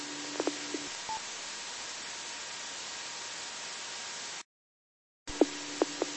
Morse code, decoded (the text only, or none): none